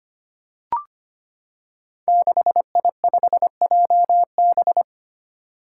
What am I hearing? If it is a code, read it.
Morse code, heard as a keyed tone, about 25 wpm, 700 Hz. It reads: 6I5JB
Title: Five Challenging Letter-Number Combinations 25wpm